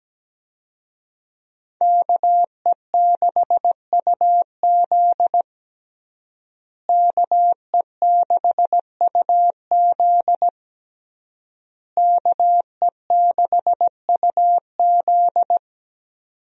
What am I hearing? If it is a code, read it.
Morse code, heard as a keyed tone, about 17 wpm, 700 Hz. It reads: KE6UZ KE6UZ KE6UZ
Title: USA Call Signs - 5 Characters - Repeated 3x - 17wpm